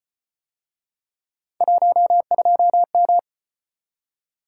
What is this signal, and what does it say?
Morse code, heard as a keyed tone, about 34 wpm, 700 Hz. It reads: 12M